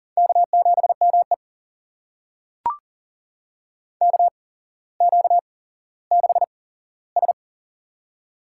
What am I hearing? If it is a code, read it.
Morse code, heard as a keyed tone, about 40 wpm, spaced as for 17 wpm, 700 Hz. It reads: K7ME KQ6S
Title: USA Call Signs - 4 Characters - 8x Character Spacing - 40wpm